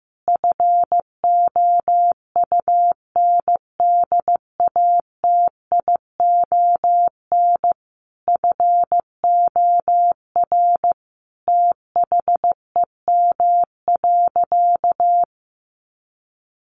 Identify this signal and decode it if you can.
Morse code, heard as a keyed tone, about 14 wpm, 700 Hz. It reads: FOUNDATION FOR THEM.
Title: Sentences from Top 1200 Words 15wpm